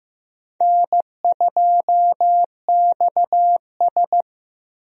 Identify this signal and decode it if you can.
Morse code, heard as a keyed tone, about 16 wpm, 700 Hz. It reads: N2XS
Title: USA Call Signs - 4 Characters - Repeated 2x - 15wpm